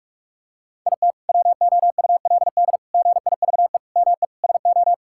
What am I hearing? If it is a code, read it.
Morse code, heard as a keyed tone, about 42 wpm, 700 Hz. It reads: IT WOULD GIVE ME SO
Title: Sentences from Top 1600 Words 45wpm